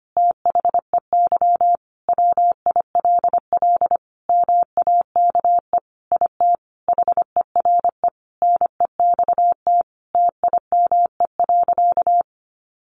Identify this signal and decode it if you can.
Morse code, heard as a keyed tone, about 25 wpm, 700 Hz. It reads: THEY WILL MAKE IT HERE NEXT TIME.
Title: Sentences from Top 1500 Words 25wpm